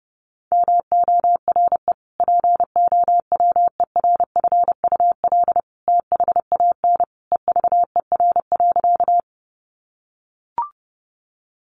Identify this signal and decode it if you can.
Morse code, heard as a keyed tone, about 30 wpm, 700 Hz. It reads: MORE POWERFUL THAN EVER.
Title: Sentences from Top 1300 Words 30wpm